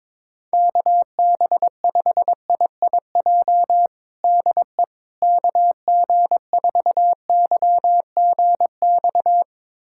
Morse code, heard as a keyed tone, about 22 wpm, 700 Hz. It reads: KB5IIJ DE KG4YGX
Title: USA Callsign DE Callsign - 6 Characters - 22wpm